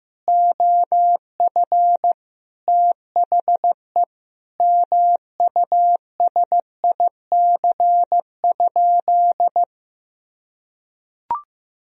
Morse code, heard as a keyed tone, about 15 wpm, 700 Hz. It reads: OF THE MUSIC?
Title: Sentences from Top 300 Words and Abbreviations 15wpm